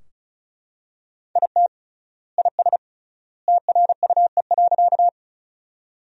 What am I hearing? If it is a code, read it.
Morse code, heard as a keyed tone, about 36 wpm, 700 Hz. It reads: IT IS TRUE.
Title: 3x Word-Spacing - Sentences from Top 400 Words - 35wpm